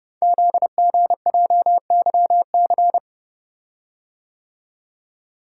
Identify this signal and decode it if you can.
Morse code, heard as a keyed tone, about 30 wpm, 690 Hz. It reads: ZGJYC
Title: Code Groups - 5 Letter:Numbers - 30wpm